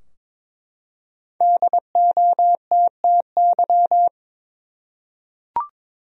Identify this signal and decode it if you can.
Morse code, heard as a keyed tone, about 21 wpm, 700 Hz. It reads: DOTTY
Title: USA Common Names - 22wpm